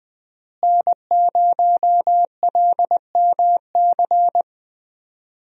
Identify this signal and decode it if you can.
Morse code, heard as a keyed tone, about 20 wpm, 700 Hz. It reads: N0LMC
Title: USA Call Signs - 5 Characters - Repeated 3x - 20wpm